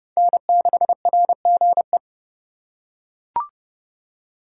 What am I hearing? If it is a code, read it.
Morse code, heard as a keyed tone, about 30 wpm, 700 Hz. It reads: N6RGE
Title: USA Call Signs - 5 Characters - 4x Character Spacing - 30wpm